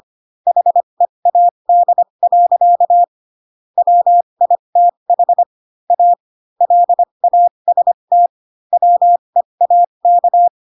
Morse code, heard as a keyed tone, about 25 wpm, spaced as for 22 wpm, 700 Hz. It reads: HEAD. WITH A LAST WEAK
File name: Pinocchio / Pinocchio ch4 18 25 ik1hge